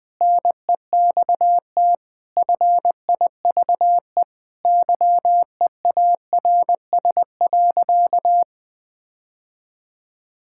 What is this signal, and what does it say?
Morse code, heard as a keyed tone, about 20 wpm, 700 Hz. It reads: NEXT FIVE YEARS.
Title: Sentences with 5-Word Phrases - 20wpm